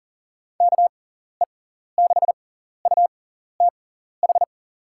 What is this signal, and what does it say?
Morse code, heard as a keyed tone, about 40 wpm, spaced as for 14 wpm, 700 Hz. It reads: KE6UTH
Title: USA Call Signs - 6 Characters - 6x Character Spacing - 40wpm